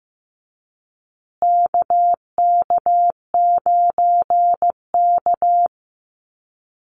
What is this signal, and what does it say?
Morse code, heard as a keyed tone, about 15 wpm, 700 Hz. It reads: KK9K
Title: USA Call Signs Repeated 3x 15wpm